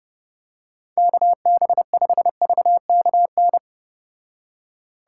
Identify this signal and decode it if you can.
Morse code, heard as a keyed tone, about 32 wpm, 700 Hz. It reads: KB5VKN